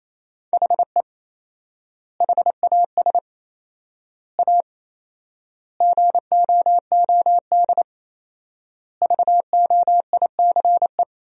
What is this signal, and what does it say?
Morse code, heard as a keyed tone, about 28 wpm, 700 Hz. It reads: HE HAS A GOOD VOICE
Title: 4x Word-Spacing - Sentences from Top 500 Words - 28wpm